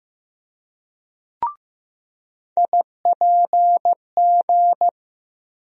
Morse code, heard as a keyed tone, about 16 wpm, 700 Hz. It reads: IPG
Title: USA Call Sign Postfixes - 3 Letters 15wpm